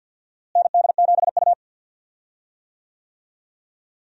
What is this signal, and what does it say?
Morse code, heard as a keyed tone, about 47 wpm, 700 Hz. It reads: ND7U
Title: USA Call Signs - 4 Characters - Repeated 3x - 50wpm